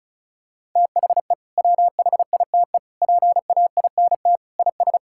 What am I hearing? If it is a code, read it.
Morse code, heard as a keyed tone, about 34 wpm, 700 Hz. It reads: THE WHITE PAINT IS
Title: Sentences from Top 1000 Words - Review Part 6 - 35wpm